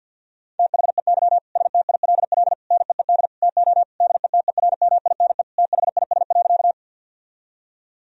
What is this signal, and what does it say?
Morse code, heard as a keyed tone, about 50 wpm, 700 Hz. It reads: THEY STILL NEED TO DETERMINE THIS.
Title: Sentences from Top 1000 Words - Review Part 4 - 50wpm